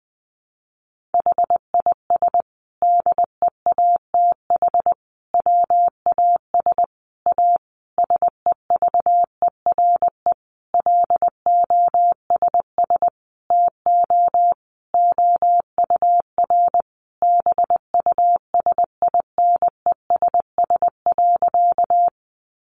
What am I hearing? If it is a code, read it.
Morse code, heard as a keyed tone, about 19 wpm, 700 Hz. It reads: HIS DEATH WAS A SEVERE LOSS TO OUR BUSINESS.
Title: Sentences from Top 1400 Words 20wpm